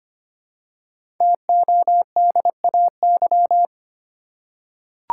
Morse code, heard as a keyed tone, about 25 wpm, 700 Hz. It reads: TODAY
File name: 5 Letter Words - First 3 Letters Given - 25wpm